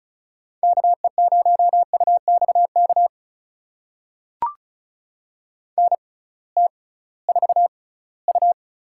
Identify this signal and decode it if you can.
Morse code, heard as a keyed tone, about 36 wpm, spaced as for 20 wpm, 700 Hz. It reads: KE0UXK NT4U
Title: USA Call Signs - 6x Character Spacing - 35wpm